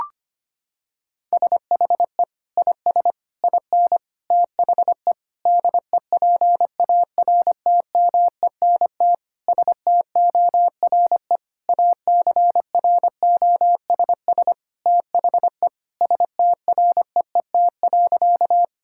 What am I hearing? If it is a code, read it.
Morse code, heard as a keyed tone, about 25 wpm, 700 Hz. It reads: SHE IS IN THE DEPARTMENT STORE ACROSS THE STREET.
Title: Sentences from Top 1000 Words - Review Part 5 - 25wpm